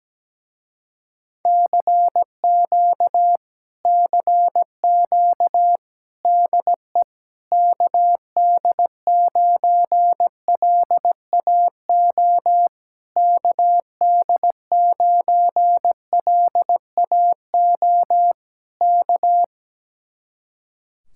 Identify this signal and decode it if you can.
Morse code, heard as a keyed tone, about 17 wpm, 700 Hz. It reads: CQ CQ DE KD9LAO KD9LAO K
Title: Speed-Racing Common QSO Phrases Encore - 17wpm